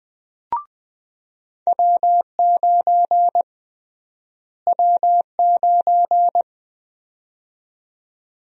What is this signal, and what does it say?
Morse code, heard as a keyed tone, about 20 wpm, 700 Hz. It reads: W9 W9
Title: USA Call Sign Prefixes - Repeated 2x - 20wpm